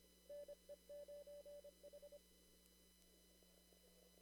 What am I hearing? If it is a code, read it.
Morse code, heard as a keyed tone, about 25 wpm, 580 Hz. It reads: NE9H
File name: CW / kt5j wpx cw-op-k5pi